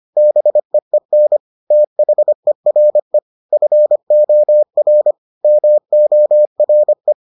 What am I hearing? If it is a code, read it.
Morse code, heard as a keyed tone, about 25 wpm, 600 Hz. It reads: BEEN THERE FOR MORE